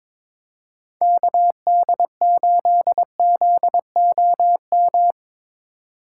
Morse code, heard as a keyed tone, about 22 wpm, 700 Hz. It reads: KD8ZOM
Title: USA Call Signs - 6 Characters - 3x Character Spacing - 22wpm